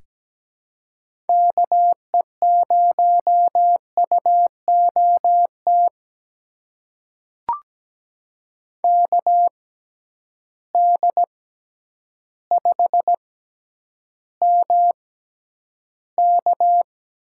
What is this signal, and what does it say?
Morse code, heard as a keyed tone, about 17 wpm, spaced as for 9 wpm, 700 Hz. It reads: KE0UOT KD5MK
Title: USA Call Signs - 6 Characters - 6x Character Spacing - 17wpm